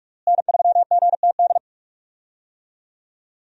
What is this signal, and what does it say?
Morse code, heard as a keyed tone, about 41 wpm, 700 Hz. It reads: N3GTD
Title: USA Call Signs - 5 Characters - Repeated 2x - 45wpm